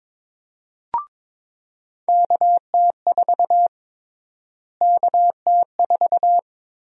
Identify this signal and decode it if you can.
Morse code, heard as a keyed tone, about 22 wpm, 700 Hz. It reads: KT4 KT4
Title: USA Call Sign Prefixes - Repeated 2x - 22wpm